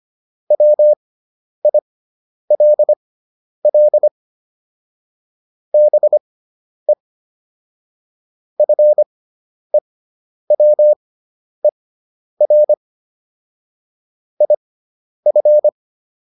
Morse code, heard as a keyed tone, about 25 wpm, spaced as for 10 wpm, 600 Hz. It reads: WILL BE FEWER IF